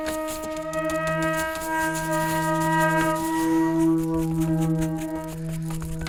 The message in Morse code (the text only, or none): none